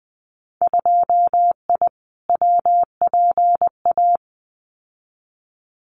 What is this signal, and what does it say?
Morse code, heard as a keyed tone, about 20 wpm, 700 Hz. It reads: 2I WPA
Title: Winter Field Day Exchanges 20wpm